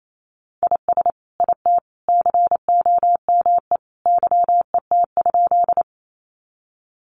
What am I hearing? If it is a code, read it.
Morse code, heard as a keyed tone, about 28 wpm, 700 Hz. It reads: IS IT COME YET?
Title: Sentences from Top 500 Words 28wpm